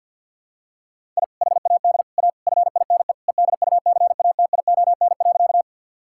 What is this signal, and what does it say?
Morse code, heard as a keyed tone, about 46 wpm, 700 Hz. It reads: I HAD A FINE EDUCATION.
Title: Sentences from Top 1000 Words - Review Part 6 - 50wpm